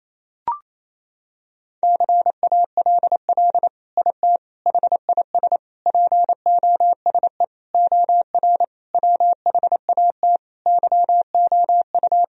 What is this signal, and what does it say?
Morse code, heard as a keyed tone, about 28 wpm, 700 Hz. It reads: CALL IT HIS POSE OR WHAT YOU
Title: Sentences from Top 1000 Words - Review Part 2 - 28wpm